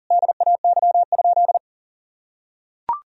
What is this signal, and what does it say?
Morse code, heard as a keyed tone, about 40 wpm, 700 Hz. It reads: DAY?
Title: Sentences from Top 1000 Words - Review Part 2 - 40wpm